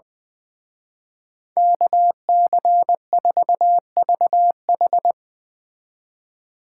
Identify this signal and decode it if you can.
Morse code, heard as a keyed tone, about 22 wpm, 700 Hz. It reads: KC4VH
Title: Speed Racing - USA 2x2 Call Signs 20wpm